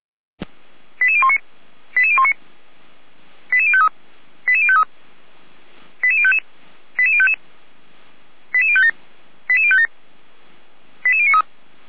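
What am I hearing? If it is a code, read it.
Five-tone selective call (ZVEI), heard as quick runs of five tones, tones of about 70 ms. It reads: 88019 88019 88053 88053 88055 88055 88057 88057 88093
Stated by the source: Five-tone ZVEI selective call received on SDR.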